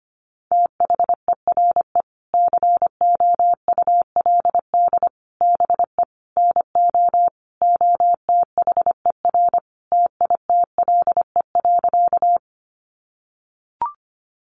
Morse code, heard as a keyed tone, about 25 wpm, 700 Hz. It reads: THERE COULD BE NO OTHER TITLE.
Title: Sentences from Top 1200 Words 25wpm